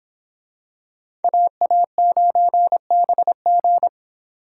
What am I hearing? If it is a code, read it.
Morse code, heard as a keyed tone, about 26 wpm, 700 Hz. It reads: AA9BG